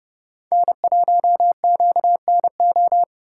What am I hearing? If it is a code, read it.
Morse code, heard as a keyed tone, about 30 wpm, 700 Hz. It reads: N1QNO
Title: USA Call Signs - 5 Characters - 3x Character Spacing - 30wpm